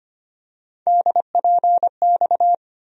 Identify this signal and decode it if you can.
Morse code, heard as a keyed tone, about 25 wpm, 700 Hz. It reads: DPX
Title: USA Call Sign Postfixes - 3 Letters 25wpm